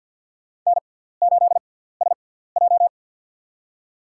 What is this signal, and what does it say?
Morse code, heard as a keyed tone, about 47 wpm, spaced as for 17 wpm, 700 Hz. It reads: N8SJ
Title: USA Call Signs - 6x Character Spacing - 50wpm